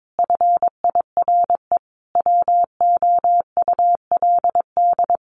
Morse code, heard as a keyed tone, about 22 wpm, 700 Hz. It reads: FIRE WOULD
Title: Sentences from Top 1500 Words 22wpm